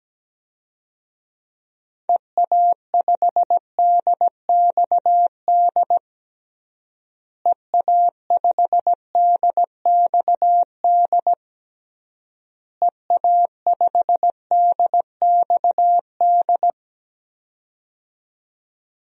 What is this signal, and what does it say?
Morse code, heard as a keyed tone, about 17 wpm, 700 Hz. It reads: EA5DXD EA5DXD EA5DXD
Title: DXCC-100 Call Signs - 6 Characters - Repeated 3x - 17wpm